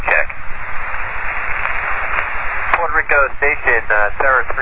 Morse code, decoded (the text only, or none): none